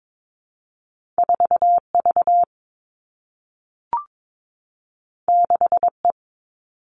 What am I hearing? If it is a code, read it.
Morse code, heard as a keyed tone, about 22 wpm, 700 Hz. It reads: 4 V 6 E
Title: Two Challenging Letter-Number Combinations 22wpm